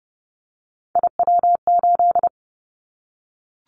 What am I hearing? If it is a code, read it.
Morse code, heard as a keyed tone, about 30 wpm, 700 Hz. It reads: IW8
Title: Callsign Prefixes for Top 25 DXCC - Repeated 3x - 30wpm